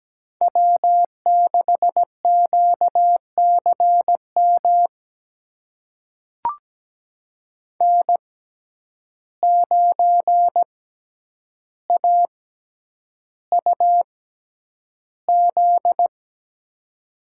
Morse code, heard as a keyed tone, about 17 wpm, spaced as for 9 wpm, 700 Hz. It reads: W6QCM N9AUZ
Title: USA Call Signs - 5 Characters - 6x Character Spacing - 17wpm